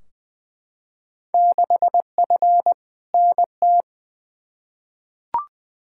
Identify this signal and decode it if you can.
Morse code, heard as a keyed tone, about 21 wpm, 700 Hz. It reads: 6F NT
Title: Field Day Exchanges Encore - 20wpm